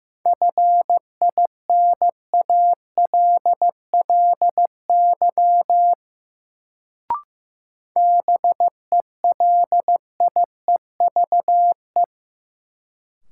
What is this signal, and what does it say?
Morse code, heard as a keyed tone, about 15 wpm, 700 Hz. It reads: FINALLY BELIEVE
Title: Sentences from Top 1000 Words - New Word Drill - 15wpm